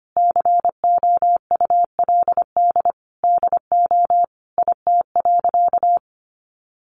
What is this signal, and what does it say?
Morse code, heard as a keyed tone, about 25 wpm, 700 Hz. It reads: COULD DO IT.
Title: Sentences from Top 1100 Words 25wpm